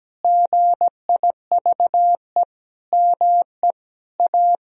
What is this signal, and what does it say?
Morse code, heard as a keyed tone, about 17 wpm, 700 Hz. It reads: GIVE ME A
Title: Sentences from Top 1000 Words - Review Part 2 - 17wpm